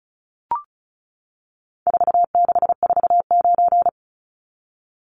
Morse code, heard as a keyed tone, about 34 wpm, 700 Hz. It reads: 4649
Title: Numbers - 4-Digit 35wpm